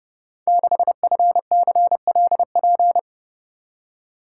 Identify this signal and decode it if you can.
Morse code, heard as a keyed tone, about 30 wpm, 700 Hz. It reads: 6FCLP
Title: Code Groups - 5 Letter:Numbers - 30wpm